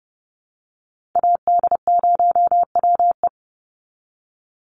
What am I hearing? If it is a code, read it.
Morse code, heard as a keyed tone, about 30 wpm, 700 Hz. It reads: AD0WE
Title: Call Signs Repeated 3x - 30wpm